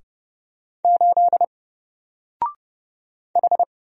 Morse code, heard as a keyed tone, about 30 wpm, 700 Hz. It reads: 8 H
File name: Single-Letter-Number-30wpm